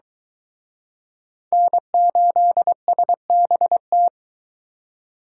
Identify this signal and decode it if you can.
Morse code, heard as a keyed tone, about 23 wpm, 700 Hz. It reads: N8SBT